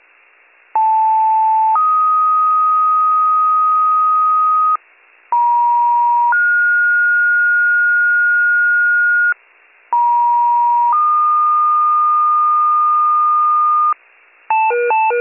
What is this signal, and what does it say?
Two-tone page, heard as a one-second tone, then a three-second tone, about 880 then 1250 Hz, about 950 then 1430 Hz, about 950 then 1220 Hz, with noise. It is heard from 0.8 to 13.9 s.